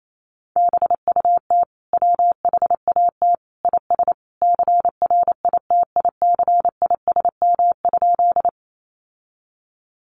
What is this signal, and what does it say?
Morse code, heard as a keyed tone, about 28 wpm, 700 Hz. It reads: BUT WHAT IS CRITICISM?
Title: Sentences from Top 1300 Words 28wpm